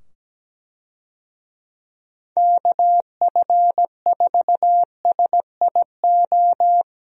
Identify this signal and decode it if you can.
Morse code, heard as a keyed tone, about 17 wpm, 700 Hz. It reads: KF4SIO
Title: USA Call Signs - 6 Characters - 17wpm